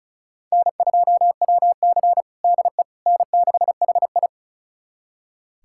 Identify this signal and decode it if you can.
Morse code, heard as a keyed tone, about 36 wpm, 700 Hz. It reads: N2WC DE N6HI
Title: USA Callsign DE Callsign - 4 Characters - 35wpm